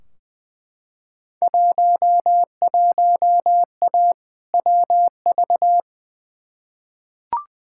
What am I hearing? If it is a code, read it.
Morse code, heard as a keyed tone, about 20 wpm, 700 Hz. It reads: 11A WV
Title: Field Day Exchanges Encore - 20wpm